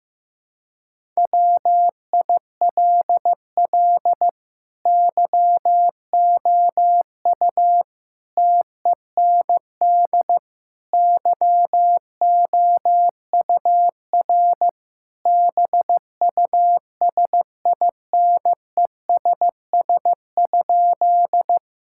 Morse code, heard as a keyed tone, about 15 wpm, 700 Hz. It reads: WILL YOU TEND YOUR BUSINESS?
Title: Sentences from Top 1000 Words - Review Part 5 - 15wpm